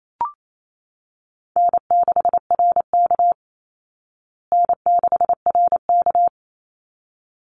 Morse code, heard as a keyed tone, about 28 wpm, 700 Hz. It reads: N6RK N6RK
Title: Easy Call Signs Repeated 2x - 28wpm